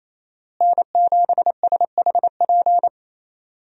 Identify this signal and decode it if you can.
Morse code, heard as a keyed tone, about 28 wpm, 700 Hz. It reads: N7SHP
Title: USA Call Signs - 5 Characters - Repeated 2x - 28wpm